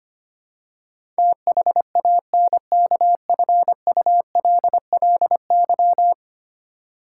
Morse code, heard as a keyed tone, about 25 wpm, 700 Hz. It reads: THANKFULLY
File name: 10 Letter Words - First 3 Letters Given - 25wpm